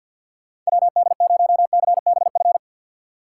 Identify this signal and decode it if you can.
Morse code, heard as a keyed tone, about 62 wpm, 700 Hz. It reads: WD0CBF
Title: Speed Racing - USA Call Signs 50wpm